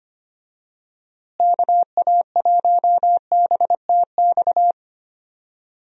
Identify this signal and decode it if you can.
Morse code, heard as a keyed tone, about 25 wpm, 700 Hz. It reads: KA1BTX
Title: USA Call Signs - 6 Characters - Repeated 2x - 25wpm